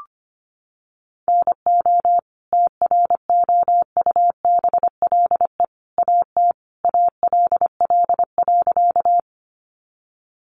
Morse code, heard as a keyed tone, about 25 wpm, 700 Hz. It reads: NO TROUBLE AT ALL.
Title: Sentences from Top 1200 Words 25wpm